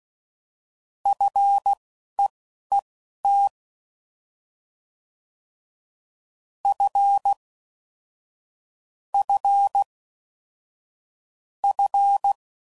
Morse code, heard as a keyed tone, about 16 wpm, spaced as for 11 wpm, 800 Hz. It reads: FEET F F F